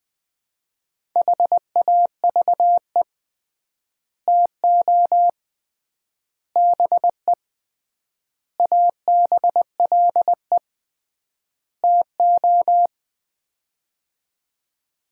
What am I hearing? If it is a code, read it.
Morse code, heard as a keyed tone, about 21 wpm, 700 Hz. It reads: HAVE TO BE ABLE TO
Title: 3x Word-Spacing - Sets of 5 Words - 20wpm